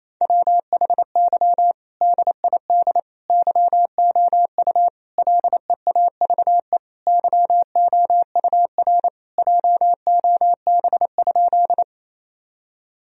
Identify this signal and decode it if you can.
Morse code, heard as a keyed tone, about 27 wpm, 700 Hz. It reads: WHY DID YOU LEAVE YOUR JOB?
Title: Sentences from Top 1000 Words - Review Part 3 - 28wpm